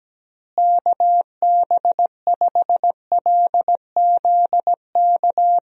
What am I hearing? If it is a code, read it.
Morse code, heard as a keyed tone, about 17 wpm, 700 Hz. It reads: KB5LZK
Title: USA Call Signs - 6 Characters - Repeated 2x - 17wpm